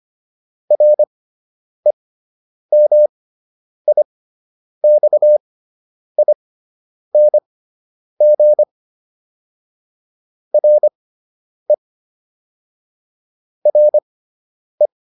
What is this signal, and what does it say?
Morse code, heard as a keyed tone, about 25 wpm, spaced as for 9 wpm, 600 Hz. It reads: REMIXING RE RE